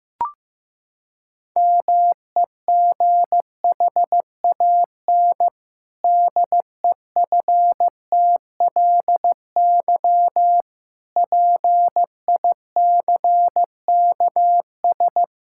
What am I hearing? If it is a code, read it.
Morse code, heard as a keyed tone, about 15 wpm, 700 Hz. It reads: MEGHAN DEFTLY PICKS
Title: Pangram Challenge v2 - 15wpm